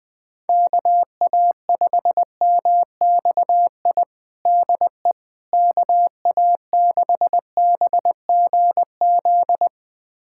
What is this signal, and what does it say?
Morse code, heard as a keyed tone, about 21 wpm, 700 Hz. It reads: KA5MXI DE KA6BGZ
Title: USA Callsign DE Callsign - 6 Characters - 20wpm